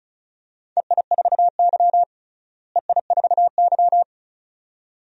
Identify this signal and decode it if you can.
Morse code, heard as a keyed tone, about 36 wpm, 700 Hz. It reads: EI4Y EI4Y
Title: DXCC-100 Call Signs - 4 Characters - Repeated 2x - 35wpm